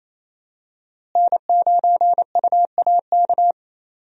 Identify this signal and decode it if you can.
Morse code, heard as a keyed tone, about 28 wpm, 700 Hz. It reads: N9UAK
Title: USA Call Signs - 5 Characters - Repeated 2x - 28wpm